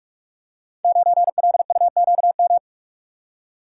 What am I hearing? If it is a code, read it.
Morse code, heard as a keyed tone, about 42 wpm, 700 Hz. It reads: 9LUQM
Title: Code Groups - 5 Letter:Numbers - 45wpm